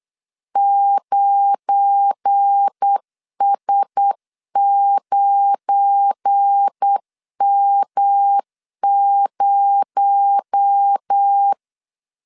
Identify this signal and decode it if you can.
Morse code, heard as a keyed tone, about 8 wpm, 800 Hz. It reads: 9S9M0
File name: LEZIONI / 9 286 8wpm 3425